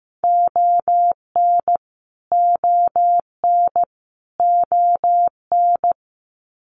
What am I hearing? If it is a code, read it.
Morse code, heard as a keyed tone, about 15 wpm, 700 Hz. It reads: ON ON ON
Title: Canadian Provinces and Territories - Repeated 3x - 15wpm